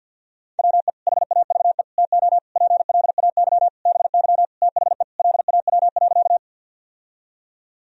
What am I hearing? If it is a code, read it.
Morse code, heard as a keyed tone, about 51 wpm, 700 Hz. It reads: WE HAVE TO PLAY BY THE LAW.
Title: Sentences from Top 1000 Words - Review Part 3 - 50wpm